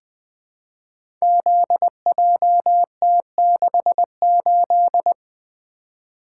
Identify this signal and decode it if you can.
Morse code, heard as a keyed tone, about 20 wpm, 700 Hz. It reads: ZJT68
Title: Code Groups - 5 Letter:Numbers - 20wpm